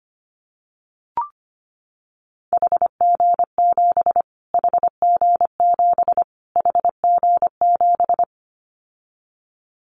Morse code, heard as a keyed tone, about 25 wpm, 700 Hz. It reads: HG7 HG7 HG7
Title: Callsign Prefixes for Top 25 DXCC - Repeated 3x - 25wpm